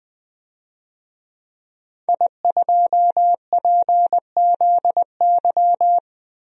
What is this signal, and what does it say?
Morse code, heard as a keyed tone, about 20 wpm, 700 Hz. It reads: I2PZY
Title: DXCC-100 Call Signs - 5 Characters - 20wpm